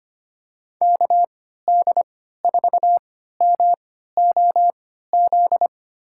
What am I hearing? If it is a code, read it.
Morse code, heard as a keyed tone, about 25 wpm, spaced as for 14 wpm, 700 Hz. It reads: KD4MOZ
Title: USA Call Signs - 6 Characters - 3x Character Spacing - 25wpm